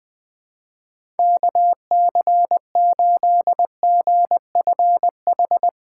Morse code, heard as a keyed tone, about 20 wpm, 700 Hz. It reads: KC8GFH
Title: Speed Racing - USA 2x3 Call Signs 20wpm